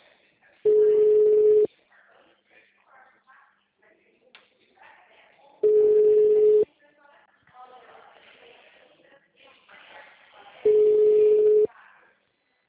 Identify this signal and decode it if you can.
Ringback tone (Europe, 425 Hz), with speech.
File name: out- -2012- - - .99351